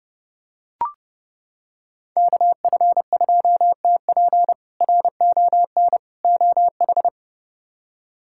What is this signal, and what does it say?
Morse code, heard as a keyed tone, about 31 wpm, 700 Hz. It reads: KF2TP RON OH
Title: K1USN SST QSO Practice - 30wpm